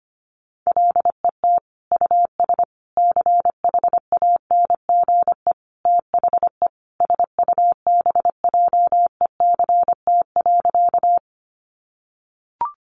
Morse code, heard as a keyed tone, about 25 wpm, 700 Hz. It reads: LET US CHANGE THE SUBJECT.
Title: Sentences from Top 800 Words 25wpm